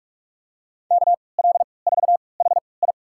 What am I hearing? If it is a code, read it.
Morse code, heard as a keyed tone, about 44 wpm, spaced as for 25 wpm, 700 Hz. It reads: KL4HI
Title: USA Call Signs - 5 Characters - 3x Character Spacing - 45wpm